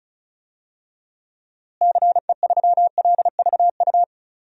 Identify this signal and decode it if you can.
Morse code, heard as a keyed tone, about 36 wpm, 700 Hz. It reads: CE3LVU
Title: DXCC-100 Call Signs - 6 Characters - Repeated 2x - 35wpm